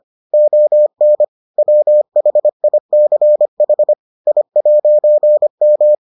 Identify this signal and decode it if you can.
Morse code, heard as a keyed tone, about 25 wpm, 600 Hz. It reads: ON WHICH I'M